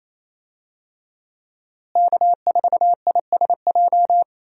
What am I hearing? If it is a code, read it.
Morse code, heard as a keyed tone, about 28 wpm, 700 Hz. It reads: K4ISJ
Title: USA Call Signs - 5 Characters - Repeated 2x - 28wpm